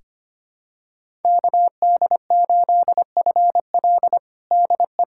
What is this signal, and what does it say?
Morse code, heard as a keyed tone, about 25 wpm, 700 Hz. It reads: KD8FL DE
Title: USA Callsign DE Callsign - 5 Characters - 25wpm